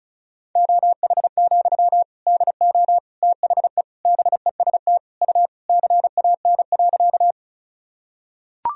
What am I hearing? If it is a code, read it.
Morse code, heard as a keyed tone, about 36 wpm, 700 Hz. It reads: OH, DO THE BEST U CAN.
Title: Sentences from Top 500 Words and Abbreviations 35wpm